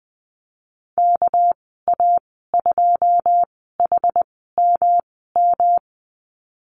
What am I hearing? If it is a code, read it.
Morse code, heard as a keyed tone, about 20 wpm, spaced as for 15 wpm, 700 Hz. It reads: KA2HMM
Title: 238 Lesson 38g - Letter q - Call Signs 20wpm